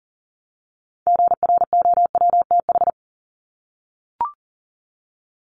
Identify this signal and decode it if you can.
Morse code, heard as a keyed tone, about 40 wpm, 700 Hz. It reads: GROWTH
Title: Top 751-1000 Words 40wpm